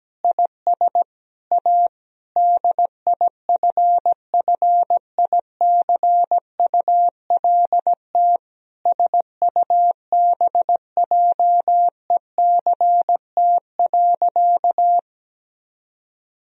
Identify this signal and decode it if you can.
Morse code, heard as a keyed tone, about 17 wpm, 700 Hz. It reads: IS A DIFFICULT SUBJECT.
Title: Sentences from Top 1000 Words - Review Part 3 - 17wpm